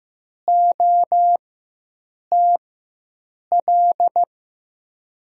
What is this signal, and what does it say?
Morse code, heard as a keyed tone, about 15 wpm, spaced as for 7 wpm, 700 Hz. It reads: OTL
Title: USA Call Signs - 6 Characters - 4x Character Spacing - 15wpm